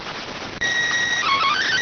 SSTV (slow-scan television), PD 290.
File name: findings / MR 2 Pel Sag T2 SSFSE 0009.gen-01 rx